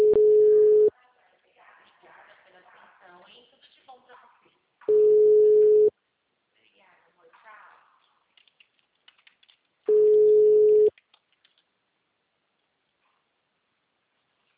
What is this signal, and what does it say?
Ringback tone (Europe, 425 Hz), with speech.